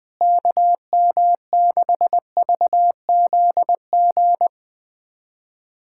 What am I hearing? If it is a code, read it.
Morse code, heard as a keyed tone, about 20 wpm, 700 Hz. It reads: KM6VZG
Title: USA Callsign DE Callsign 20wpm